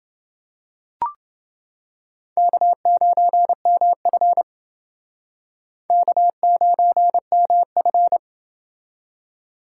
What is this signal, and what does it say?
Morse code, heard as a keyed tone, about 28 wpm, 700 Hz. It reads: K9MF K9MF